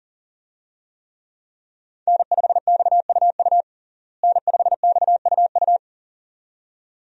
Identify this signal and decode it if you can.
Morse code, heard as a keyed tone, about 40 wpm, 700 Hz. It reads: N5XUU N5XUU
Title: USA Call Signs - 5 Characters - Repeated 2x - 40wpm